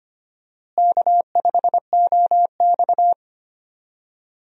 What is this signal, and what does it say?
Morse code, heard as a keyed tone, about 25 wpm, 700 Hz. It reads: K5OX
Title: USA Call Signs - 4 Characters - Repeated 3x - 25wpm